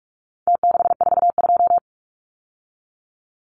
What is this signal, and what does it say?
Morse code, heard as a keyed tone, about 44 wpm, 700 Hz. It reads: T642
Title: Numbers - Contest Style - 4-Digit 45wpm